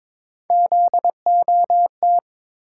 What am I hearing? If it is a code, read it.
Morse code, heard as a keyed tone, about 22 wpm, 700 Hz. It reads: ZOT
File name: practice / 22wpm / Three Letter-Number Combinations - 22wpm